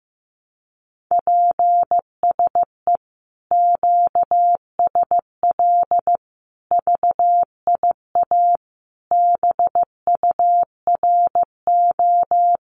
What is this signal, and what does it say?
Morse code, heard as a keyed tone, about 15 wpm, 700 Hz. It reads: PSE QSL VIA BURO
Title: Common QSO Phrases Encore 15wpm